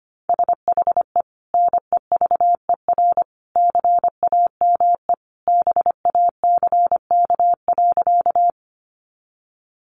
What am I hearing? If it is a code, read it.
Morse code, heard as a keyed tone, about 25 wpm, 700 Hz. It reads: SHE NEVER CAME BACK.
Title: Sentences from Top 300 Words 25wpm